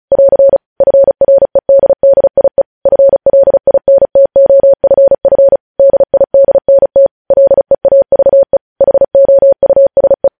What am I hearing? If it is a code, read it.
Morse code, heard as a keyed tone, about 36 wpm, 550 Hz. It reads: <AR> FREDDIE FLINTOFF DIDNT LEAVE HOUSE
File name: podcast / News headlines-1x-35x35